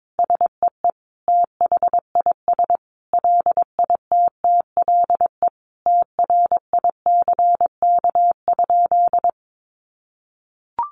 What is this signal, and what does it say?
Morse code, heard as a keyed tone, about 22 wpm, 700 Hz. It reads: SEE THIS LITTLE TRICK?
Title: Sentences from Top 1500 Words 22wpm